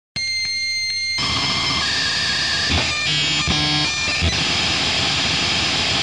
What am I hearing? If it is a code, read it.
Dial-up modem handshake.